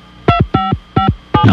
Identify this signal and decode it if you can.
DTMF, heard as telephone touch tones, tones of about 120 ms. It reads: 2557